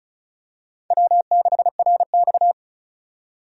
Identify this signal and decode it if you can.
Morse code, heard as a keyed tone, about 36 wpm, 700 Hz. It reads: W6RX
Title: USA Call Signs - 4 Characters - 35wpm